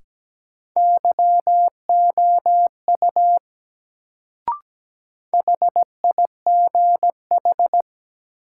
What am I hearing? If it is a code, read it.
Morse code, heard as a keyed tone, about 17 wpm, 700 Hz. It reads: YOU HIGH
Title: Sets of 2 Words - New Word Drill - 17wpm